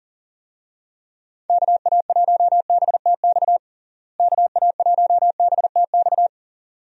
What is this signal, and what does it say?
Morse code, heard as a keyed tone, about 40 wpm, 700 Hz. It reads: KA1BTX KA1BTX
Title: USA Call Signs - 6 Characters - Repeated 2x - 40wpm